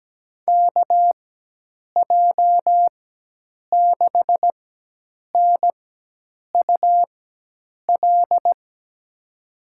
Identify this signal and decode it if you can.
Morse code, heard as a keyed tone, about 17 wpm, spaced as for 8 wpm, 700 Hz. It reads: KJ6NUL
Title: USA Call Signs - 6 Characters - 4x Character Spacing - 17wpm